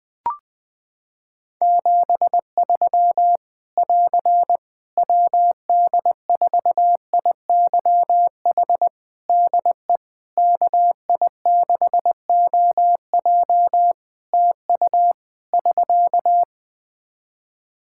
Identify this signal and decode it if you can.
Morse code, heard as a keyed tone, about 20 wpm, 700 Hz. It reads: 73 <AR> WD4IYH DE KI6OJ TU <SK>
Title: K1USN SST QSO Practice 20wpm